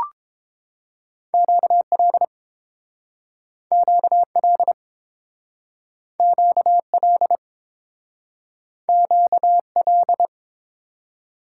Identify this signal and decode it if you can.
Morse code, heard as a keyed tone, about 30 wpm, 700 Hz. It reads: QL QL QL QL